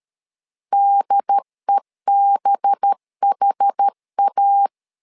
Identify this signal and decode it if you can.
Morse code, heard as a keyed tone, about 13 wpm, 800 Hz. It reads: DEBHA
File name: LEZIONI / 5 LBVC / 5 LBVC 12wpm 1955